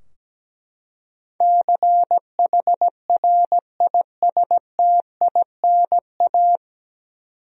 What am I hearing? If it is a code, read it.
Morse code, heard as a keyed tone, about 17 wpm, 700 Hz. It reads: CHRISTINA
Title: USA Common Names - 17wpm